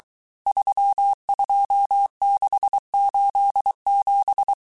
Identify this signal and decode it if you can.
Morse code, heard as a keyed tone, about 23 wpm, 800 Hz. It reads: 32687
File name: cours-du-mercredi / audios / cours-855-texte-chiffre-vitesse-22-mpm